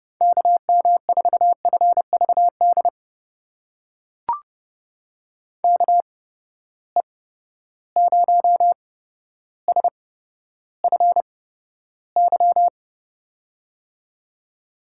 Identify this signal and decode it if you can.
Morse code, heard as a keyed tone, about 30 wpm, spaced as for 13 wpm, 700 Hz. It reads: KM4FVD KE0SFY
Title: USA Call Signs - 6 Characters - 8x Character Spacing - 30wpm